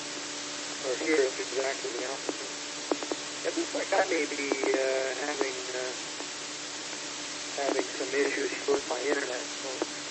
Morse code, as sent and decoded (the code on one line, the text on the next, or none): none